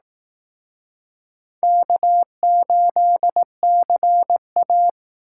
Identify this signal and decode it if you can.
Morse code, heard as a keyed tone, about 18 wpm, 700 Hz. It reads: K8CA
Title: Speed Racing - USA 1x2 Call Signs 15wpm